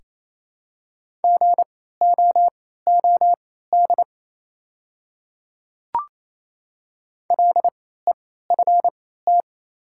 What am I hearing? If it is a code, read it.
Morse code, heard as a keyed tone, about 28 wpm, spaced as for 16 wpm, 700 Hz. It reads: GOOD LEFT
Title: 3x Character Spacing - Top 300 Words - 28wpm